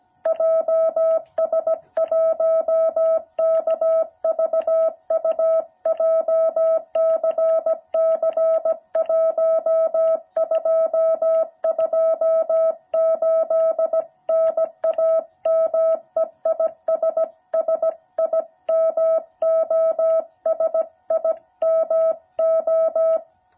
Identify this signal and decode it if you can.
Morse code, heard as a keyed tone, about 17 wpm, 650 Hz. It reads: JS1KVU JCC 1228 NAME IS SIMO SIMO